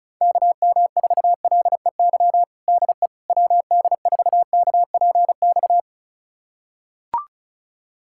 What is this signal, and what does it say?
Morse code, heard as a keyed tone, about 36 wpm, 700 Hz. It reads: KM4LEY DE WD4KPX
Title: USA Callsign DE Callsign 35wpm